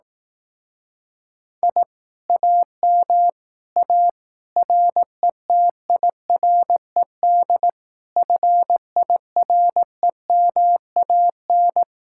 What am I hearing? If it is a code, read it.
Morse code, heard as a keyed tone, about 18 wpm, 700 Hz. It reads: I AM A RETIRED FIREMAN
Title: Speed-Racing Common QSO Phrases Encore - 15wpm